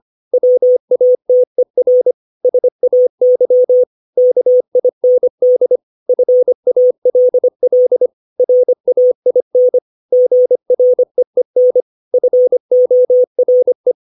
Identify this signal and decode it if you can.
Morse code, heard as a keyed tone, about 25 wpm, 500 Hz. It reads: WATER SAY KIND FALL RAIN GREEN FORE
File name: 632 words / 632 Words 25WPM